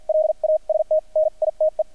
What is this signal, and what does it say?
Morse code, heard as a keyed tone, about 50 wpm, spaced as for 41 wpm, 620 Hz, with radio noise. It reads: LAST NITE
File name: archived / CRSnet